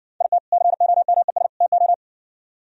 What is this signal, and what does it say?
Morse code, heard as a keyed tone, about 59 wpm, 700 Hz. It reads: IT COMES TO